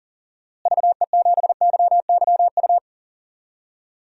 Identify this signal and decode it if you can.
Morse code, heard as a keyed tone, about 40 wpm, 700 Hz. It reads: VE7YYU
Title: DXCC-100 Call Signs - 6 Characters - Repeated 2x - 40wpm